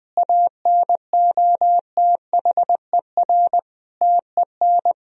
Morse code, heard as a keyed tone, about 20 wpm, 700 Hz. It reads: ANOTHER TEN
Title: Sentences from Top 1000 Words - Review Part 4 - 20wpm